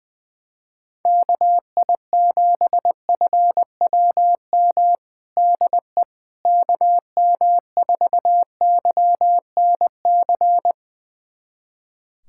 Morse code, heard as a keyed tone, about 20 wpm, 700 Hz. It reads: KI7FWM DE KM4YNC
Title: USA Callsign DE Callsign - 6 Characters - 20wpm